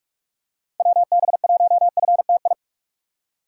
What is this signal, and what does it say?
Morse code, heard as a keyed tone, about 41 wpm, 700 Hz. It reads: WB1FTI
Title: USA Call Signs - 6 Characters - 4x Character Spacing - 45wpm